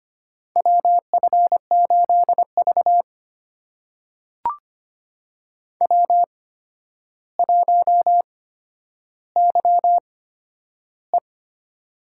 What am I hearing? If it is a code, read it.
Morse code, heard as a keyed tone, about 25 wpm, spaced as for 11 wpm, 700 Hz. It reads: WF8V W1YE
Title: USA Call Signs - 4 Characters - 8x Character Spacing - 25wpm